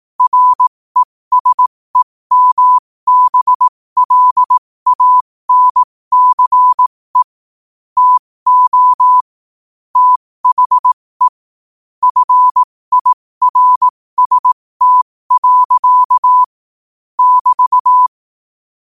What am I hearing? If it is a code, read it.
Morse code, heard as a keyed tone, about 18 wpm, spaced as for 15 wpm, 1000 Hz. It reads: RESEMBLANCE TO THE FIRST. <BT>
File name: qotdfiles / 15wpm